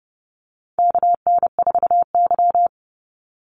Morse code, heard as a keyed tone, about 30 wpm, 700 Hz. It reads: KN4Y
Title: Call Signs Repeated 2x - 30wpm